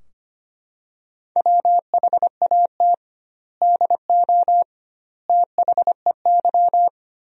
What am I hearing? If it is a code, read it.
Morse code, heard as a keyed tone, about 25 wpm, 700 Hz. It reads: WHAT DO THEY
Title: 2x Word-Spacing - Sentences from Top 100 Words - 25wpm